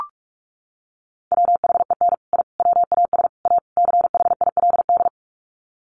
Morse code, heard as a keyed tone, about 44 wpm, 700 Hz. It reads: WHEN I WAS A CHILD
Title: Sets of 5 Words Encore 45wpm